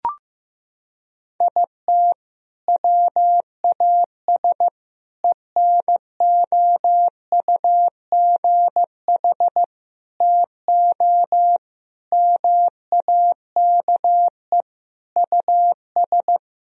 Morse code, heard as a keyed tone, about 15 wpm, 700 Hz. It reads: IT WAS ENOUGH TO MAKE US
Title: Sentences from Top 1000 Words - Review Part 4 - 15wpm